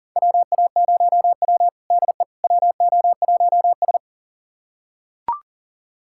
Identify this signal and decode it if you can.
Morse code, heard as a keyed tone, about 37 wpm, 700 Hz. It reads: WA0W DE WO1S
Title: USA Callsign DE Callsign - 4 Characters - 40wpm